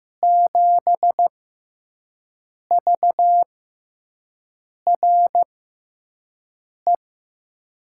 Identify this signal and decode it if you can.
Morse code, heard as a keyed tone, about 15 wpm, spaced as for 5 wpm, 700 Hz. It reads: 7VRE
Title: USA Call Signs - 6 Characters - 6x Character Spacing - 15wpm